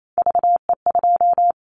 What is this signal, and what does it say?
Morse code, heard as a keyed tone, about 28 wpm, 700 Hz. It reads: VE2
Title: Callsign Prefixes for Top 25 DXCC - Repeated 3x - 28wpm